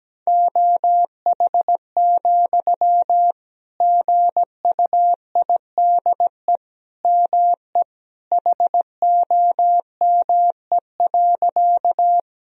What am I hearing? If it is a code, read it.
Morse code, heard as a keyed tone, about 17 wpm, 700 Hz. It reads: OH, GUIDE ME HOME.
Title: Sentences from Top 1000 Words - Review Part 3 - 17wpm